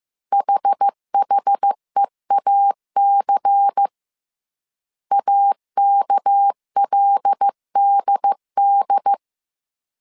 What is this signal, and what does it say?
Morse code, heard as a keyed tone, about 15 wpm, 800 Hz. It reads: HHEAC AKLDD
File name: CW / DET / 1 6 1500C S C 14wpm 2951